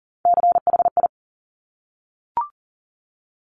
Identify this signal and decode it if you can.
Morse code, heard as a keyed tone, about 41 wpm, 700 Hz. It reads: CHI
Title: Top Three Letter Combinations in Words 40wpm